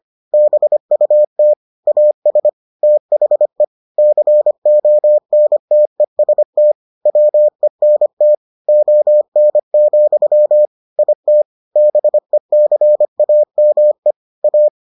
Morse code, heard as a keyed tone, about 25 wpm, 600 Hz. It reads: BUT AS THE CONTEST WENT ON, IT BECAME A